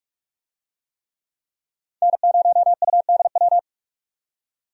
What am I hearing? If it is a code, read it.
Morse code, heard as a keyed tone, about 46 wpm, 700 Hz. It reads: N0UDW
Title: USA Call Signs - 5 Characters - 45wpm